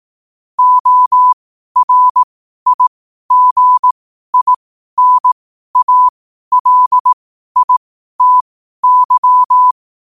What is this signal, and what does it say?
Morse code, heard as a keyed tone, about 18 wpm, spaced as for 12 wpm, 1000 Hz. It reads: ORIGINALITY
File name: qotdfiles / 12wpm